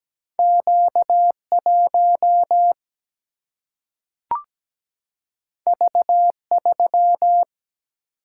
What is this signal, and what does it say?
Morse code, heard as a keyed tone, about 17 wpm, 700 Hz. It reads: Q1 V3
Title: Single-Letter Single-Number 17wpm